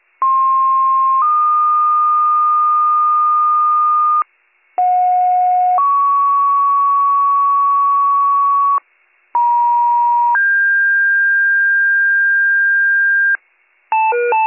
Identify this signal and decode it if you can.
Two-tone page, heard as a one-second tone, then a three-second tone, about 1060 then 1220 Hz, about 730 then 1120 Hz, about 950 then 1600 Hz. It is heard from 0.2 to 13.4 s.